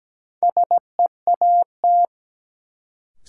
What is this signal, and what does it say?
Morse code, heard as a keyed tone, about 17 wpm, 700 Hz. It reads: SEAT
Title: Sentences from Top 1500 Words - New Word Drill - 17wpm